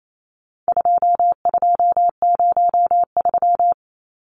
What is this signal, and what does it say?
Morse code, heard as a keyed tone, about 28 wpm, 700 Hz. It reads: 2203
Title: Numbers - 4-Digit 28wpm